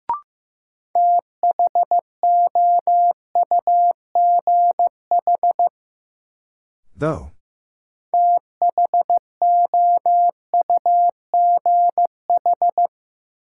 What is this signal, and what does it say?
Morse code, heard as a keyed tone, about 15 wpm, 700 Hz. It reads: THOUGH THOUGH
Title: Sentences from Top 400 Words - New Word Drill - 15wpm